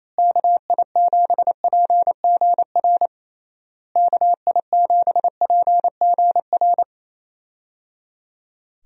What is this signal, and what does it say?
Morse code, heard as a keyed tone, about 28 wpm, 700 Hz. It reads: KI7PGR KI7PGR
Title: USA Call Signs - 6 Characters - Repeated 3x - 28wpm